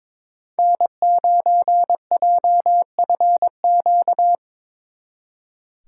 Morse code, heard as a keyed tone, about 25 wpm, 700 Hz. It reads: N9JFQ
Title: Speed Racing - USA 1x3 Call Signs 22wpm